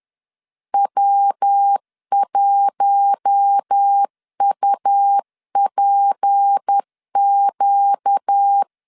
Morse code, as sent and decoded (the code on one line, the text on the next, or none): .-- .---- ..- .--. --.-
W1UPQ